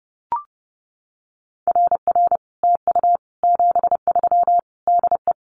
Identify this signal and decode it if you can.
Morse code, heard as a keyed tone, about 30 wpm, 700 Hz. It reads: RR TU 73 DE
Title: Common QSO Phrases Encore 30wpm